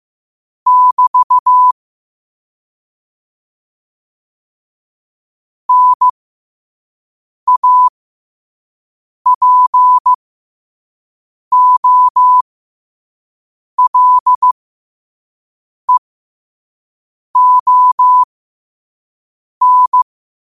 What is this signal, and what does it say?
Morse code, heard as a keyed tone, about 15 wpm, spaced as for 5 wpm, 1000 Hz. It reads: <BT> NAPOLEON
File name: qotdfiles / 5wpm